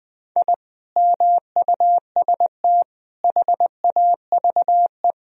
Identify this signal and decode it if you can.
Morse code, heard as a keyed tone, about 21 wpm, 700 Hz. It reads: I MUST HAVE
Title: Sentences from Top 1000 Words - Review Part 2 - 20wpm